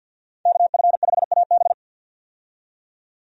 Morse code, heard as a keyed tone, about 53 wpm, 700 Hz. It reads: KF5AB
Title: USA Call Signs - 5 Characters - 50wpm